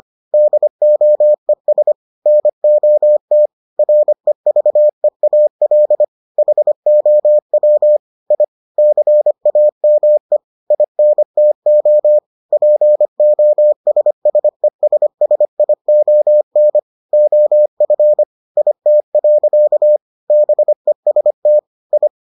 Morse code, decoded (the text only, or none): DOES NOT REVEAL HOW I CAME INTO POSSESSION OF IT. BEST I